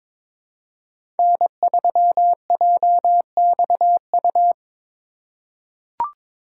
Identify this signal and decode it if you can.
Morse code, heard as a keyed tone, about 22 wpm, 700 Hz. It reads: N3JXU
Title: USA Call Signs - 5 Characters - 2x Character Spacing - 22wpm